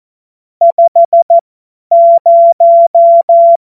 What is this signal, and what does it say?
Morse code, heard as a keyed tone, about 14 wpm, spaced as for 10 wpm, 680 Hz. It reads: 50
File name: Morsekurs / Witze / Witz 30. Dezember 2025 50 F70